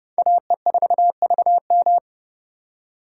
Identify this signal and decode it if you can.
Morse code, heard as a keyed tone, about 30 wpm, 700 Hz. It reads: AE4VM